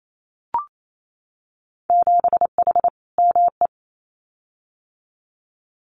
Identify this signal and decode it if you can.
Morse code, heard as a keyed tone, about 28 wpm, 700 Hz. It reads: 7H ME
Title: Winter Field Day Exchanges Encore 28wpm